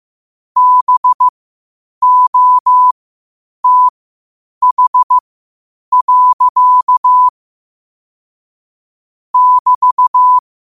Morse code, heard as a keyed tone, about 15 wpm, spaced as for 8 wpm, 1000 Hz. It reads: BOTH. <BT>
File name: qotdfiles / 8wpm